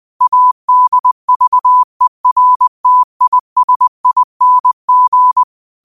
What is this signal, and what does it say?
Morse code, heard as a keyed tone, about 20 wpm, 1000 Hz. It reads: ADVERTISING